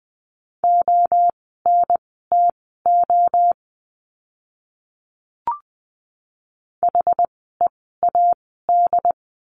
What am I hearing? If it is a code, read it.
Morse code, heard as a keyed tone, about 20 wpm, spaced as for 14 wpm, 700 Hz. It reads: ONTO HEAD
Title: 068 Lesson 12f - Letter d - Words 20wpm